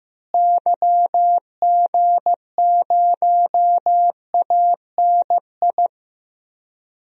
Morse code, heard as a keyed tone, about 15 wpm, 700 Hz. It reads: YG0ANI
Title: DXCC-100 Call Signs - 6 Characters - Repeated 3x - 15wpm